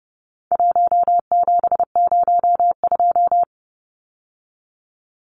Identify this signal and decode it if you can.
Morse code, heard as a keyed tone, about 30 wpm, 700 Hz. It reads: 1702
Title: Numbers - 4-Digit 30wpm